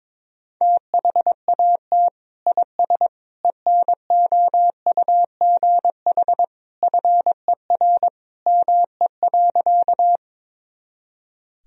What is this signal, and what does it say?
Morse code, heard as a keyed tone, about 22 wpm, 700 Hz. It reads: THAT IS ENOUGH FER ME.
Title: Sentences from Top 400 Words and Abbreviations 22wpm